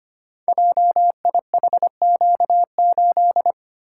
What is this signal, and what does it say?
Morse code, heard as a keyed tone, about 25 wpm, 700 Hz. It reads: JIHQ8
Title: Code Groups - 5 Letter:Numbers - 25wpm